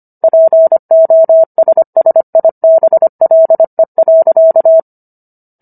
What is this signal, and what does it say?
Morse code, heard as a keyed tone, about 25 wpm, 650 Hz. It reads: POSSIBLE.